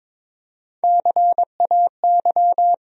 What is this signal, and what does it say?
Morse code, heard as a keyed tone, about 22 wpm, 700 Hz. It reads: CAY
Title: USA Call Sign Postfixes - 3 Letters 22wpm